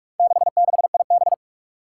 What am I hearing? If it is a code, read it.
Morse code, heard as a keyed tone, about 44 wpm, 700 Hz. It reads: 66IB
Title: Four Challenging Letter-Number Combinations 45wpm